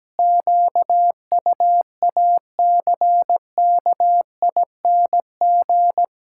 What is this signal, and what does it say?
Morse code, heard as a keyed tone, about 17 wpm, 700 Hz. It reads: QUACKING
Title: Pangram Challenge v2 - 17wpm